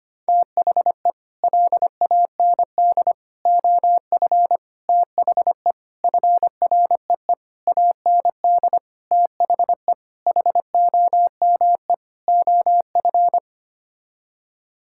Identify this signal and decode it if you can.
Morse code, heard as a keyed tone, about 25 wpm, 700 Hz. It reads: THE LAND OF THE FREE AND THE HOME OF
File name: Sets of 9 Words - 25wpm